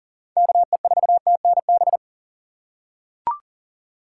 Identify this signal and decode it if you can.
Morse code, heard as a keyed tone, about 40 wpm, 700 Hz. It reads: KE4TNB
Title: USA Call Signs - 6 Characters - 3x Character Spacing - 40wpm